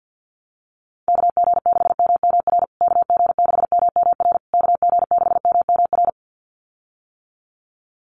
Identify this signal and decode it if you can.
Morse code, heard as a keyed tone, about 50 wpm, 700 Hz. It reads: KG6MMR KG6MMR KG6MMR
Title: USA Call Signs Repeated 3x 50wpm